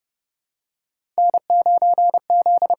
Morse code, heard as a keyed tone, about 31 wpm, 700 Hz. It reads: N 9 Z
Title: USA Call Signs - 4 Characters - 8x Character Spacing - 30wpm